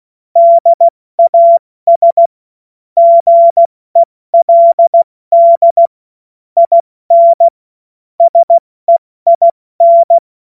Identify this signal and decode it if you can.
Morse code, heard as a keyed tone, about 16 wpm, 680 Hz. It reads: DAS GELD IN SEIN
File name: Morsekurs / Witze / Witz 06. September 2022 70 F80